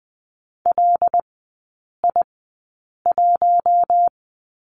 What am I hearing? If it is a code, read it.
Morse code, heard as a keyed tone, about 20 wpm, spaced as for 8 wpm, 700 Hz. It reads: LI1
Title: 074 Lesson 13e - Letter L - Three Characters 20wpm